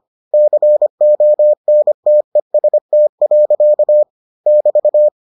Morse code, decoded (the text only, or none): CONTEST. <BT>